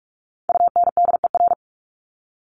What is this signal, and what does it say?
Morse code, heard as a keyed tone, about 44 wpm, 700 Hz. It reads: UNDER
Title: Top 300 Words 45wpm